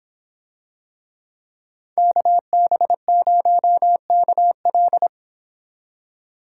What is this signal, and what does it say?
Morse code, heard as a keyed tone, about 26 wpm, 700 Hz. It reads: KB0KL